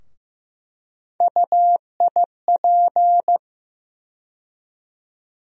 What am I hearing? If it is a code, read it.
Morse code, heard as a keyed tone, about 15 wpm, 700 Hz. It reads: UIP
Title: USA Call Sign Postfixes - 3 Letters 15wpm